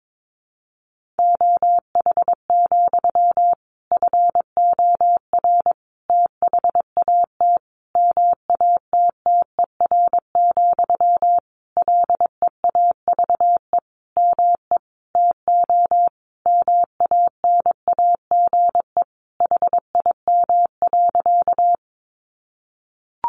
Morse code, heard as a keyed tone, about 22 wpm, 700 Hz. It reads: OH, FOR THAT MATTER, LEAVE ME TO MANAGE HIM.
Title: Sentences from Top 1400 Words 22wpm